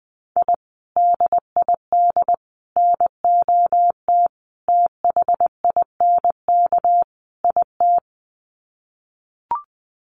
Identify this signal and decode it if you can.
Morse code, heard as a keyed tone, about 20 wpm, 700 Hz. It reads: I DID NOT THINK IT
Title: Sets of 5 Words Encore2 20wpm